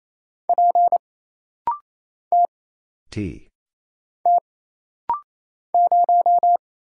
Morse code, heard as a keyed tone, about 28 wpm, 700 Hz. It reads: P T T 0
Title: Single Letter-Number - Rapid-Fire 28wpm